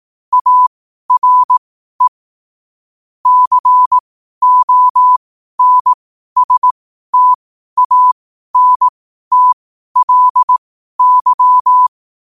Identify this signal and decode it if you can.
Morse code, heard as a keyed tone, about 18 wpm, spaced as for 12 wpm, 1000 Hz. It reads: ARE CONSTANTLY